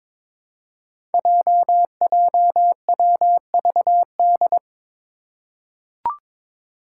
Morse code, heard as a keyed tone, about 22 wpm, 700 Hz. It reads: JJWVD
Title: Code Groups - 5 Letters - 22wpm